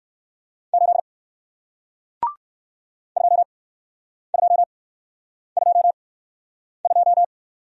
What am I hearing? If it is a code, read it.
Morse code, heard as a keyed tone, about 55 wpm, spaced as for 9 wpm, 700 Hz. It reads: 8 2222
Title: Speed-Racing Single Letter-Number 45wpm